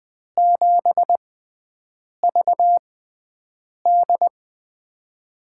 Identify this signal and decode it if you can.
Morse code, heard as a keyed tone, about 20 wpm, spaced as for 7 wpm, 700 Hz. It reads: 7VD
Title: USA Call Signs - 4 Characters - 6x Character Spacing - 20wpm